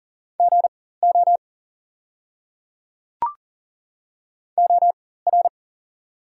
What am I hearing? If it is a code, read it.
Morse code, heard as a keyed tone, about 37 wpm, spaced as for 18 wpm, 700 Hz. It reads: GO OR
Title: 4x Character Spacing - Top 200 Words - 40wpm